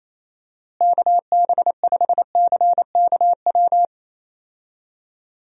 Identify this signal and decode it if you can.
Morse code, heard as a keyed tone, about 28 wpm, 700 Hz. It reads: KB5CKW
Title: USA Call Signs - 6 Characters - Repeated 2x - 28wpm